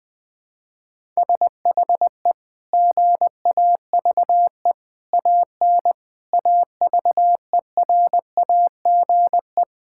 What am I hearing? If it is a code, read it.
Morse code, heard as a keyed tone, about 19 wpm, 700 Hz. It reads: SHE GAVE AN AVERAGE
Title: Sentences from Top 1000 Words - Review Part 6 - 20wpm